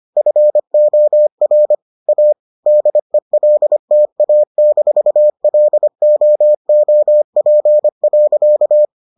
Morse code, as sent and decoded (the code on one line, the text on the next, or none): ..-. --- .-. / .- / -.. .-.. - .- -....- .-.. --- --- .--. .-.-.-
FOR A DELTA-LOOP.